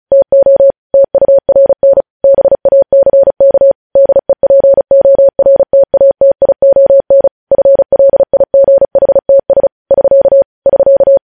Morse code, heard as a keyed tone, about 36 wpm, 550 Hz. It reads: TO TURN BACK DEPORTATION FLIGHTS <SK> <SK>
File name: podcast / News headlines-1x-35x35